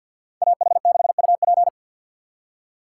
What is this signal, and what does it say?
Morse code, heard as a keyed tone, about 58 wpm, 700 Hz. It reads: AH6UP
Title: Speed Racing - USA 2x2 Call Signs 50wpm